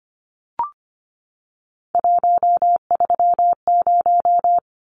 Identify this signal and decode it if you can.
Morse code, heard as a keyed tone, about 25 wpm, 700 Hz. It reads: 130
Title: Numbers - 3-Digit 25wpm